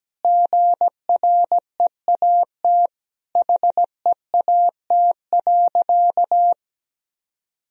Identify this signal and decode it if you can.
Morse code, heard as a keyed tone, about 17 wpm, 700 Hz. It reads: GREAT HEAT.
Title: Sentences from Top 1000 Words - Review Part 3 - 17wpm